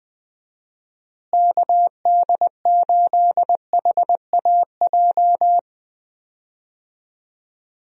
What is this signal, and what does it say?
Morse code, heard as a keyed tone, about 20 wpm, 700 Hz. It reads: KD8HAJ
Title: USA Call Signs - 6 Characters - Repeated 3x - 20wpm